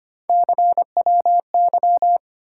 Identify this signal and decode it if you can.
Morse code, heard as a keyed tone, about 25 wpm, 700 Hz. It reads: CWY
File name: Three Letter-Number Combinations - 25wpm